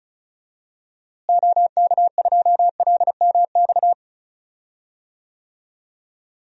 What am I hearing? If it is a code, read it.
Morse code, heard as a keyed tone, about 36 wpm, 700 Hz. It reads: OK2LMX
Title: DXCC-100 Call Signs - 6 Characters - Repeated 3x - 35wpm